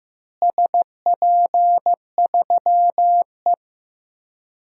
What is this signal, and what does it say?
Morse code, heard as a keyed tone, about 15 wpm, 700 Hz. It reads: SP3E
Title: DXCC-100 Call Signs - 4 Characters - Repeated 2x - 15wpm